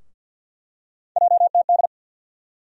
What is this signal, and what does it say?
Morse code, heard as a keyed tone, about 47 wpm, 690 Hz. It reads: JTD
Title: USA Call Sign Postfixes - 3 Letters 50wpm